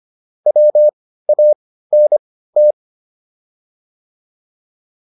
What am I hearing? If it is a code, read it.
Morse code, heard as a keyed tone, about 25 wpm, spaced as for 15 wpm, 600 Hz. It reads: WANT